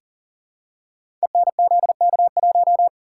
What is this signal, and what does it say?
Morse code, heard as a keyed tone, about 39 wpm, 700 Hz. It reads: ENZK1
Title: Code Groups - 5 Letter:Numbers - 40wpm